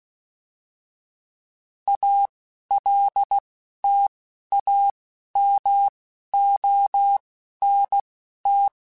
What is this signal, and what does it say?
Morse code, heard as a keyed tone, about 16 wpm, spaced as for 11 wpm, 800 Hz. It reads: ALTAMONT